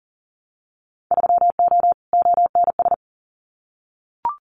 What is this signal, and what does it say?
Morse code, heard as a keyed tone, about 38 wpm, 700 Hz. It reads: 3O ONS
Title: Winter Field Day Exchanges 40wpm